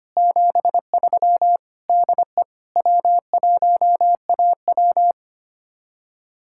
Morse code, heard as a keyed tone, about 30 wpm, 700 Hz. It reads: 73 DE W1AW
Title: Speed-Racing Common QSO Phrases Encore - 25wpm